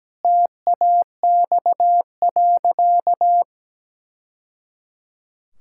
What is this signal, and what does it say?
Morse code, heard as a keyed tone, about 17 wpm, 700 Hz. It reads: TAX.
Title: Sentences from Top 1000 Words - Review Part 5 - 17wpm